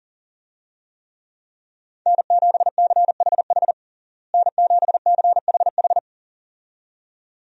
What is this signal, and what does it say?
Morse code, heard as a keyed tone, about 37 wpm, 700 Hz. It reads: N7CHH N7CHH
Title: USA Call Signs - 5 Characters - Repeated 2x - 40wpm